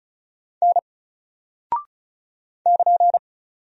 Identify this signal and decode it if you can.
Morse code, heard as a keyed tone, about 34 wpm, 700 Hz. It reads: N <KN>
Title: Single Character (Letter:Number:Punctuation:Prosign) 35wpm